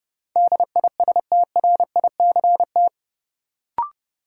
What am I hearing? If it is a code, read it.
Morse code, heard as a keyed tone, about 31 wpm, 700 Hz. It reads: DISTRICT
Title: Sentences from Top 1600 Words - New Word Drill Alt - 30wpm